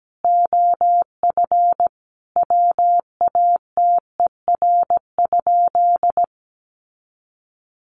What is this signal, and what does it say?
Morse code, heard as a keyed tone, about 17 wpm, 700 Hz. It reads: OF WATER?
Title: Sentences from Top 1300 Words 17wpm